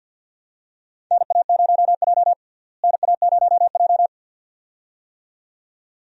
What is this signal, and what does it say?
Morse code, heard as a keyed tone, about 48 wpm, 700 Hz. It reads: NA0J NA0J
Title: USA Call Signs - 4 Characters - Repeated 2x - 50wpm